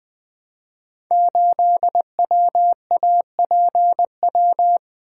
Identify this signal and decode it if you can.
Morse code, heard as a keyed tone, about 20 wpm, 700 Hz. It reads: 8WAPW
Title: Code Groups - 5 Letter:Numbers - 20wpm